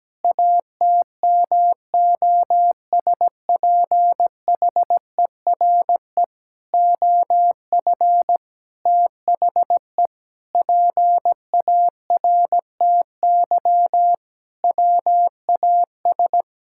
Morse code, heard as a keyed tone, about 17 wpm, 700 Hz. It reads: ATMOSPHERE OF THE PARTY WAS
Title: Sentences from Top 1600 Words 17wpm